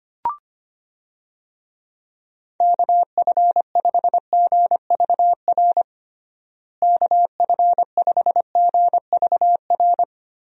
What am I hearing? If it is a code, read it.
Morse code, heard as a keyed tone, about 25 wpm, 700 Hz. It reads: KF5GVR KF5GVR
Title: USA Call Signs - 6 Characters - Repeated 3x - 25wpm